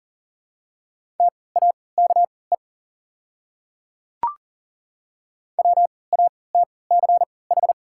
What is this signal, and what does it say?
Morse code, heard as a keyed tone, about 40 wpm, spaced as for 23 wpm, 700 Hz. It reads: TAKE WATCH
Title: 3x Character Spacing - Top 500 Words - 40wpm